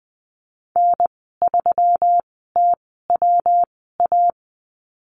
Morse code, heard as a keyed tone, about 20 wpm, spaced as for 15 wpm, 700 Hz. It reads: N3TWA
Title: 167 Lesson 27g - Number 7 - Call Signs 20wpm